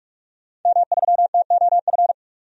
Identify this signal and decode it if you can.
Morse code, heard as a keyed tone, about 42 wpm, 700 Hz. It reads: M3TOF
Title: DXCC-100 Call Signs - 5 Characters - 45wpm